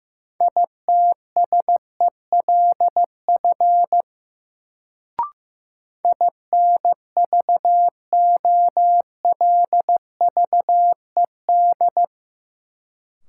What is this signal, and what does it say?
Morse code, heard as a keyed tone, about 15 wpm, 700 Hz. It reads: ITSELF INVOLVED
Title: Sentences from Top 800 Words - New Word Drill - 15wpm